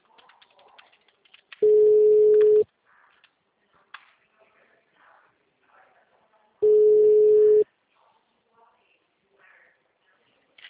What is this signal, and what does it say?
Ringback tone (Europe, 425 Hz), with speech.
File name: out- -2008- - - .0